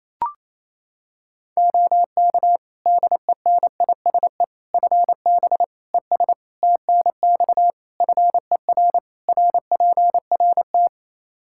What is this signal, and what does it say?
Morse code, heard as a keyed tone, about 28 wpm, 700 Hz. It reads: OK DENISE FB ES TNX FER RPRT
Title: Practice QSO - Exchange 2 - Info - 28wpm